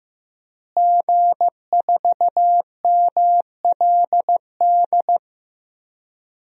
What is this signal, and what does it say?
Morse code, heard as a keyed tone, about 15 wpm, 700 Hz. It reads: G4MLD
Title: DXCC-100 Call Signs - 5 Characters - Repeated 3x - 15wpm